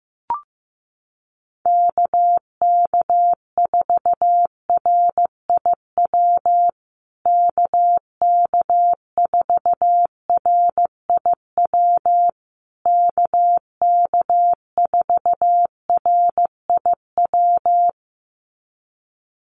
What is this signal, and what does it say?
Morse code, heard as a keyed tone, about 15 wpm, 700 Hz. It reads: KK4RIW KK4RIW KK4RIW
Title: USA Call Signs Repeated 3x 15wpm